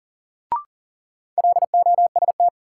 Morse code, heard as a keyed tone, about 40 wpm, 700 Hz. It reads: LOST
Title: Sentences from Top 1400 Words - New Word Drill Alt - 40wpm